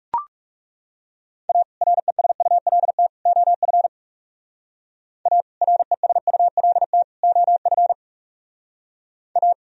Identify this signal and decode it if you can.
Morse code, heard as a keyed tone, about 37 wpm, 700 Hz. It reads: A RESULT OF A RESULT OF A
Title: Speed-Racing - Sets of 3 Words 30wpm